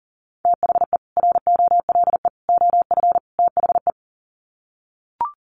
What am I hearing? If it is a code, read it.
Morse code, heard as a keyed tone, about 40 wpm, 700 Hz. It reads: THE ROLE OF THE
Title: Sets of 4 Words 40wpm